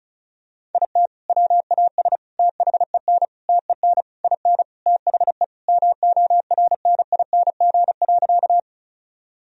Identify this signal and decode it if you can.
Morse code, heard as a keyed tone, about 36 wpm, 700 Hz. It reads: IT WAS THEN TEN IN THE MORNING.
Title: Sentences from Top 1000 Words - Review Part 2 - 35wpm